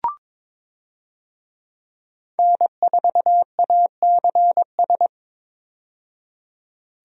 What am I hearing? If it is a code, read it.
Morse code, heard as a keyed tone, about 22 wpm, 700 Hz. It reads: N4ACS
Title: USA Call Signs - 5 Characters - 22wpm